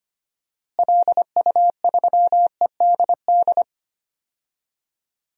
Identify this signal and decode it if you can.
Morse code, heard as a keyed tone, about 25 wpm, 700 Hz. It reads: LU3EDD
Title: DXCC-100 Call Signs - 6 Characters - Repeated 2x - 25wpm